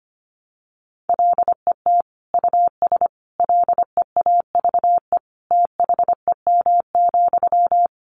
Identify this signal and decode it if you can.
Morse code, heard as a keyed tone, about 25 wpm, 700 Hz. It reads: LET US LEAVE THEM,
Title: Sentences from Top 400 Words 25wpm